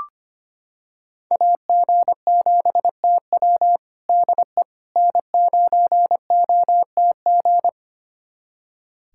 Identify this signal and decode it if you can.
Morse code, heard as a keyed tone, about 25 wpm, 700 Hz. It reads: AG7TW DE N9OTG
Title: USA Callsign DE Callsign - 5 Characters - 25wpm